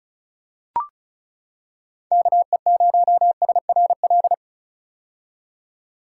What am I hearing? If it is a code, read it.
Morse code, heard as a keyed tone, about 36 wpm, 700 Hz. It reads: KE0SRL
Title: K1USN SST QSO Practice - 35wpm